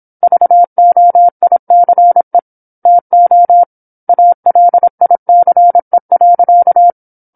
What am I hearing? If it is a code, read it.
Morse code, heard as a keyed tone, about 27 wpm, 700 Hz. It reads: VOICE TO ALICE.